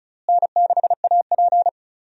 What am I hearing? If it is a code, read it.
Morse code, heard as a keyed tone, about 34 wpm, 700 Hz. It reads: N6AP
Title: USA Call Signs - 4 Characters - Repeated 2x - 35wpm